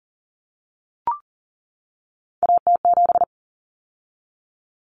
Morse code, heard as a keyed tone, about 37 wpm, 700 Hz. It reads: AT7
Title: Common Cut Numbers - 3-Digit 40wpm